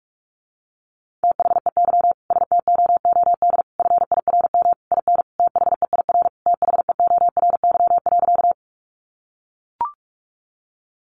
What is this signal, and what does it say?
Morse code, heard as a keyed tone, about 44 wpm, 700 Hz. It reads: THEY STOOD FIRM IN THEIR THEORY.
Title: Sentences from Top 1400 Words 45wpm